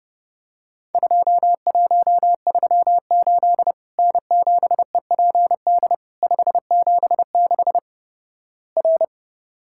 Morse code, heard as a keyed tone, about 29 wpm, 700 Hz. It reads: 2138 N7EPD 576 R
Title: Worked All Europe DX Contest - QTC Exchange - 30wpm